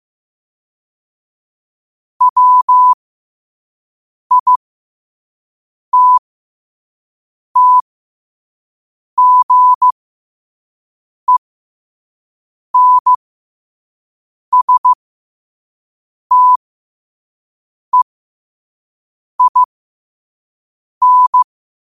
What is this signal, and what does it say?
Morse code, heard as a keyed tone, about 15 wpm, spaced as for 5 wpm, 1000 Hz. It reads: WITTGENSTEIN
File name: 5wpm